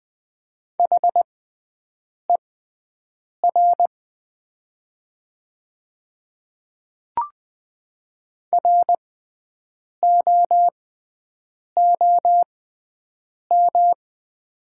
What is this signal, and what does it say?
Morse code, heard as a keyed tone, about 20 wpm, spaced as for 7 wpm, 700 Hz. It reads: HER ROOM
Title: 6x Character Spacing - Top 300 Words - 20wpm